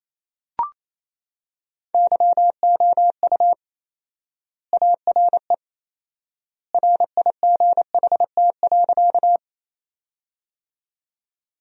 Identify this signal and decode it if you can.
Morse code, heard as a keyed tone, about 28 wpm, 700 Hz. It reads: YOU ARE RIGHT.
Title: 4x Word-Spacing - Sentences from Top 200 Words - 28wpm